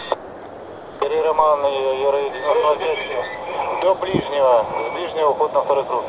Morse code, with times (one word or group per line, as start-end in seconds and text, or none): none